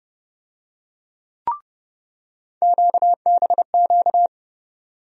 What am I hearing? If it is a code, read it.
Morse code, heard as a keyed tone, about 30 wpm, 700 Hz. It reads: QBQ
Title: USA Call Sign Postfixes - 3 Letters 30wpm